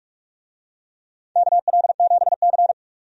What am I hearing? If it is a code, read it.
Morse code, heard as a keyed tone, about 45 wpm, 700 Hz. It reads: KL7C
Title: DXCC-100 Call Signs - 4 Characters - 45wpm